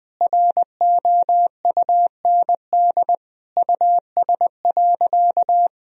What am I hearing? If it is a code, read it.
Morse code, heard as a keyed tone, about 19 wpm, 700 Hz. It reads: ROUND US.
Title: Sentences from Top 1000 Words - Review Part 1 - 20wpm